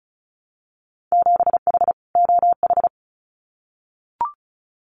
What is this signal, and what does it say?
Morse code, heard as a keyed tone, about 34 wpm, 700 Hz. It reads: 7H OH
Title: Winter Field Day Exchanges Encore 35wpm